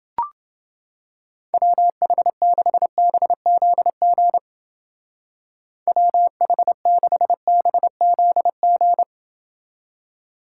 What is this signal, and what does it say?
Morse code, heard as a keyed tone, about 29 wpm, 700 Hz. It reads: WH6BZG WH6BZG